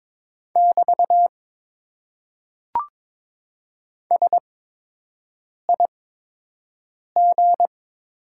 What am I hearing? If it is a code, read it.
Morse code, heard as a keyed tone, about 22 wpm, spaced as for 6 wpm, 700 Hz. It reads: <BT> SIG
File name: Top 90 QSO Elements - 8x Character Spacing - 22wpm